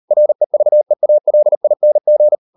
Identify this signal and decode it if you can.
Morse code, heard as a keyed tone, about 38 wpm, 600 Hz. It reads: REVEALING